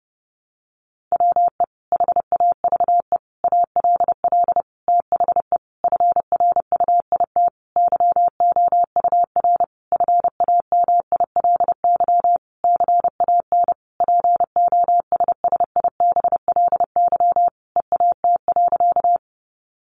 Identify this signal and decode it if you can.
Morse code, heard as a keyed tone, about 30 wpm, 700 Hz. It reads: WE HAVE ALL THE FRUIT YOUR FAMILY CAN POSSIBLY EAT.
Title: Sentences from Top 1300 Words 30wpm